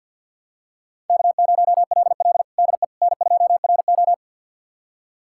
Morse code, heard as a keyed tone, about 51 wpm, 700 Hz. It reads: K0LL DE N2RO
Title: USA Callsign DE Callsign - 4 Characters - 50wpm